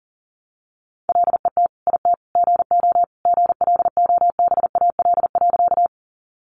Morse code, heard as a keyed tone, about 40 wpm, 700 Hz. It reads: LET IT GO GLOBAL.
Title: Sentences from Top 1200 Words 40wpm